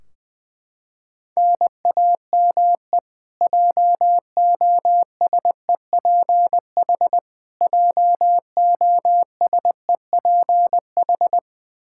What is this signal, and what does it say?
Morse code, heard as a keyed tone, about 20 wpm, 700 Hz. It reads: NAME JOSEPH JOSEPH
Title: K1USN SST QSO Practice 20wpm